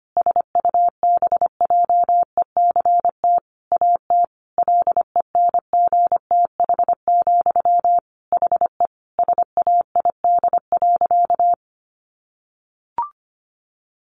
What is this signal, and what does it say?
Morse code, heard as a keyed tone, about 25 wpm, 700 Hz. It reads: SUBJECT AT LENGTH, HE SAID.
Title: Sentences from Top 1000 Words 25wpm